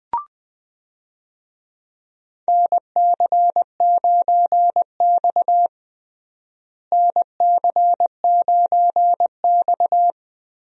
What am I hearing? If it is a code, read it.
Morse code, heard as a keyed tone, about 21 wpm, 700 Hz. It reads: NC9X NC9X
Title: USA Call Signs - 4 Characters - Repeated 3x - 20wpm